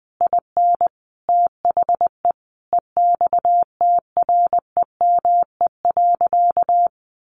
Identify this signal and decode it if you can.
Morse code, heard as a keyed tone, about 20 wpm, 700 Hz. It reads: IN THE EXTREME.
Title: Sentences from Top 1300 Words 20wpm